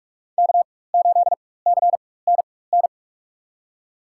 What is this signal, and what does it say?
Morse code, heard as a keyed tone, about 44 wpm, spaced as for 21 wpm, 700 Hz. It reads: K8CNN
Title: USA Call Signs - 4x Character Spacing - 45wpm